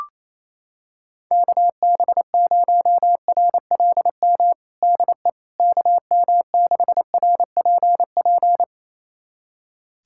Morse code, heard as a keyed tone, about 27 wpm, 700 Hz. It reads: KB0RLM DE KM6RPP
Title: USA Callsign DE Callsign - 6 Characters - 28wpm